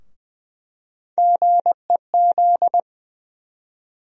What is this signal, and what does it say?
Morse code, heard as a keyed tone, about 20 wpm, 700 Hz. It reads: GEZ
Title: USA Call Sign Postfixes - 3 Letters 20wpm